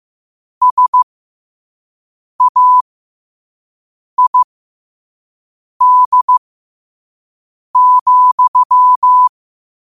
Morse code, heard as a keyed tone, about 15 wpm, spaced as for 5 wpm, 1000 Hz. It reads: SAID,
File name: qotdfiles / 5wpm